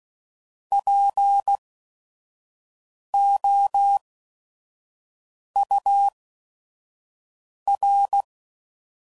Morse code, heard as a keyed tone, about 16 wpm, spaced as for 5 wpm, 800 Hz. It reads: POUR
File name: Prefixes / By First Letter / Ps